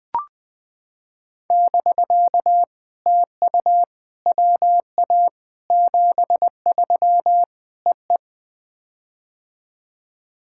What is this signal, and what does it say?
Morse code, heard as a keyed tone, about 21 wpm, 700 Hz. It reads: <BK> TU WA 73 EE
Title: POTA QSO Practice - 20wpm